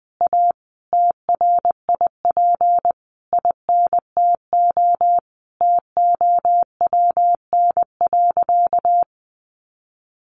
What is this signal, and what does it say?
Morse code, heard as a keyed tone, about 19 wpm, 700 Hz. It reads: A TRIP INTO TOWN.
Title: Sentences from Top 1300 Words 20wpm